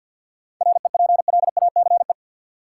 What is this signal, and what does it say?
Morse code, heard as a keyed tone, about 52 wpm, 700 Hz. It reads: REPLACE
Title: Sentences from Top 1400 Words - New Word Drill Alt - 50wpm